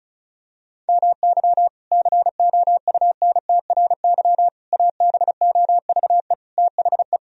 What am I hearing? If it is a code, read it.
Morse code, heard as a keyed tone, about 36 wpm, 700 Hz. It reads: MY COUNTRY ABOVE THE
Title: Sentences from Top 1000 Words - Review Part 2 - 35wpm